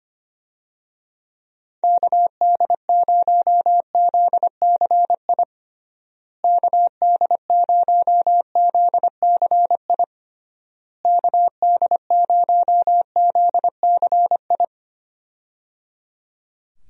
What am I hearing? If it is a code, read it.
Morse code, heard as a keyed tone, about 25 wpm, 700 Hz. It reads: KD0ZCI KD0ZCI KD0ZCI
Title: USA Call Signs - 6 Characters - Repeated 3x - 25wpm